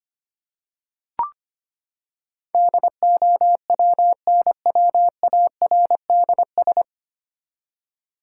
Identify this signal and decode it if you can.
Morse code, heard as a keyed tone, about 25 wpm, 700 Hz. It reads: DOWNWARDS
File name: Nine Letter Words - 25wpm